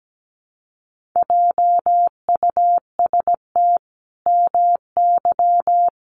Morse code, heard as a keyed tone, about 17 wpm, 700 Hz. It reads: JUST MY
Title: Sentences from Top 1200 Words 17wpm